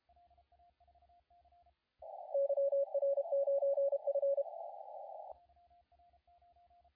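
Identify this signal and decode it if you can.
Morse code, heard as a keyed tone, about 32 wpm, 560 Hz, with radio noise. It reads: YR9F
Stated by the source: CW contest audio recorded at LA4C during the SAC CW contest.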